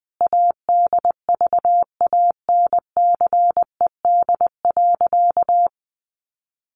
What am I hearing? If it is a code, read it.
Morse code, heard as a keyed tone, about 20 wpm, 700 Hz. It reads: ADVANCED.
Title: Sentences from Top 1200 Words 20wpm